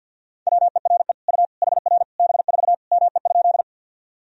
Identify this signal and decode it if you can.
Morse code, heard as a keyed tone, about 50 wpm, 700 Hz. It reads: WERE U HR B4 ME?
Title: Sentences from Top 200 Words and Abbreviations 50wpm